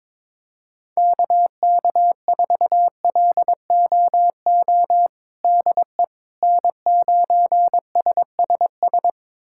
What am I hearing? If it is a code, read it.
Morse code, heard as a keyed tone, about 22 wpm, 700 Hz. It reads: KK4LOO DE N9SSS
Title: USA Callsign DE Callsign 22wpm